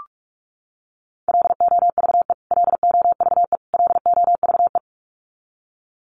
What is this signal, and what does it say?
Morse code, heard as a keyed tone, about 44 wpm, 700 Hz. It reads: LOVE LOVE LOVE
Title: Top 500 Words Repeated 45wpm